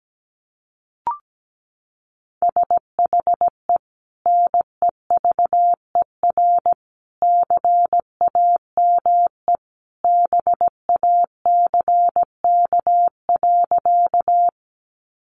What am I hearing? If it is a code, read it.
Morse code, heard as a keyed tone, about 17 wpm, 700 Hz. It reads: SHE NEVER CAME BACK.
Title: Sentences from Top 300 Words 17wpm